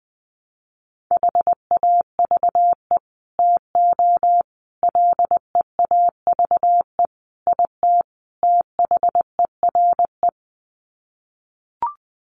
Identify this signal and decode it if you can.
Morse code, heard as a keyed tone, about 20 wpm, 700 Hz. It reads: HAVE TO LEAVE IT THERE
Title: Sets of 5 Words Encore2 20wpm